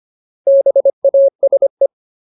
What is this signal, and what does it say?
Morse code, heard as a keyed tone, about 25 wpm, 550 Hz. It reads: BASE
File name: Ultimate Word List / 25WPM Ultimate